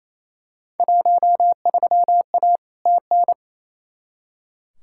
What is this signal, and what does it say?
Morse code, heard as a keyed tone, about 28 wpm, 700 Hz. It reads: 13A TN
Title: Field Day Exchanges Encore - 28wpm